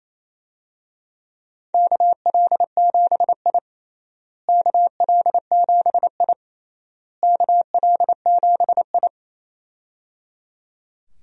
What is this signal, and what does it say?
Morse code, heard as a keyed tone, about 28 wpm, 700 Hz. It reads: KL7I KL7I KL7I
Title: DXCC-100 Call Signs - 4 Characters - Repeated 3x - 28wpm